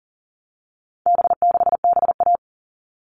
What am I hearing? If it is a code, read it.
Morse code, heard as a keyed tone, about 40 wpm, 700 Hz. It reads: B6BA
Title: All Cut Numbers - 4-Digit 40wpm